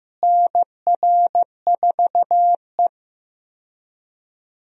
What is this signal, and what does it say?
Morse code, heard as a keyed tone, about 15 wpm, 700 Hz. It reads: NR4E
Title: USA Call Signs - 4 Characters - Repeated 3x - 15wpm